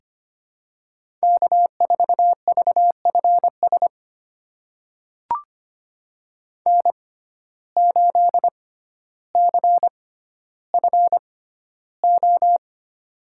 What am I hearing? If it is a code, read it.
Morse code, heard as a keyed tone, about 25 wpm, spaced as for 13 wpm, 700 Hz. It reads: K4VFS N8CFO
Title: USA Call Signs - 5 Characters - 6x Character Spacing - 25wpm